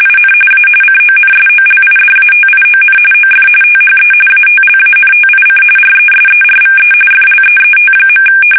RTTY, about 45.45 baud. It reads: IN A CONSOLIDATION OF THE ATLANTIC RESERVE FLEET.
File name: fsk850 / battleshipnj